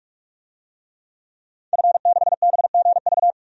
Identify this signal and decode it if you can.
Morse code, heard as a keyed tone, about 44 wpm, 700 Hz. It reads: F6BGV
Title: DXCC-100 Call Signs - 5 Characters - 45wpm